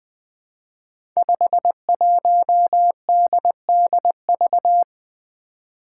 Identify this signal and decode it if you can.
Morse code, heard as a keyed tone, about 20 wpm, 700 Hz. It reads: 51DDV
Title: Five Challenging Letter-Number Combinations 20wpm